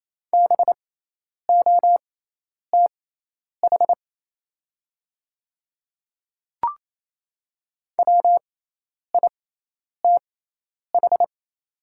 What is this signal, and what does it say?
Morse code, heard as a keyed tone, about 28 wpm, spaced as for 10 wpm, 690 Hz. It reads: BOTH WITH
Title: 6x Character Spacing - Top 500 Words - 28wpm